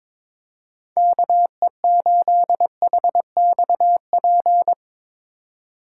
Morse code, heard as a keyed tone, about 22 wpm, 700 Hz. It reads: KE8HXP
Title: USA Call Signs - 2x Character Spacing - 22wpm